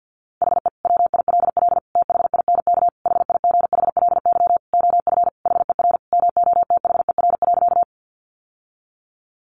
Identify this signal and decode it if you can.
Morse code, heard as a keyed tone, about 50 wpm, 700 Hz. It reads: HE WILL THINK HIGHLY OF HER MOTHER.
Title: Sentences from Top 1100 Words 50wpm